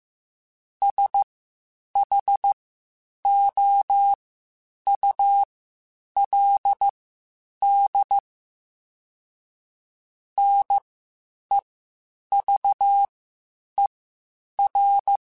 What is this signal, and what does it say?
Morse code, heard as a keyed tone, about 15 wpm, spaced as for 8 wpm, 800 Hz. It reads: SHOULD NEVER
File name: Quotes Spaced / Quotes Spaced-mp3s / Advice sp @13